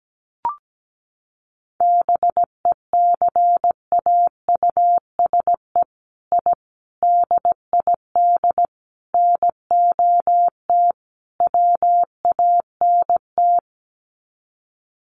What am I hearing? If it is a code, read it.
Morse code, heard as a keyed tone, about 17 wpm, 700 Hz. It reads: BECAUSE I DID NOT WANT
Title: Sets of 5 Words Encore2 17wpm